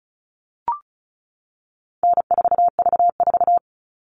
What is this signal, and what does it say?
Morse code, heard as a keyed tone, about 34 wpm, 700 Hz. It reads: N4V4
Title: All Cut Numbers - 4-Digit 35wpm